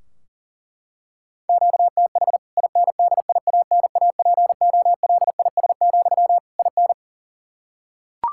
Morse code, heard as a keyed tone, about 40 wpm, 700 Hz. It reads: QTH INDIANAPOLIS, IN
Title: K1USN SST QSO Practice 40wpm